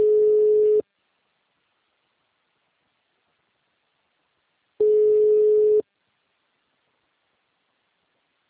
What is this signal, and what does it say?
Ringback tone (Europe, 425 Hz).
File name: out- -2014- - - .16597